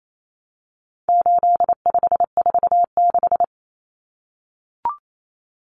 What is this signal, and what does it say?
Morse code, heard as a keyed tone, about 28 wpm, 700 Hz. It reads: 8546
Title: Common Cut Numbers - 4-Digit 28wpm